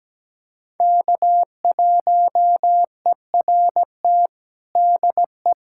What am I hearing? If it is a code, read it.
Morse code, heard as a keyed tone, about 17 wpm, 700 Hz. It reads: K1ERT DE
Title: USA Callsign DE Callsign - 5 Characters - 17wpm